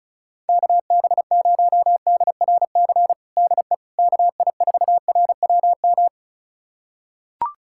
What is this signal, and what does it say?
Morse code, heard as a keyed tone, about 34 wpm, 700 Hz. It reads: KB0DRC DE KI4RWM
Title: USA Callsign DE Callsign - 6 Characters - 35wpm